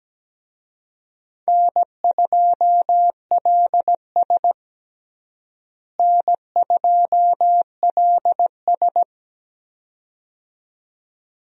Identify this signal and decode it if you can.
Morse code, heard as a keyed tone, about 17 wpm, 700 Hz. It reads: N2LS N2LS
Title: USA Call Signs - 4 Characters - Repeated 2x - 17wpm